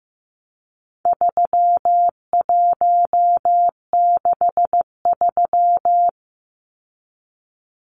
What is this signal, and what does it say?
Morse code, heard as a keyed tone, about 15 wpm, 700 Hz. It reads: 3163
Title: Numbers - 4-Digit 15wpm